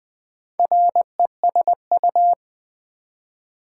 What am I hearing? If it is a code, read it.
Morse code, heard as a keyed tone, about 20 wpm, 700 Hz. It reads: RESU
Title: Build A Word - 2 Letters At A Time - 6-Letter Words - 20wpm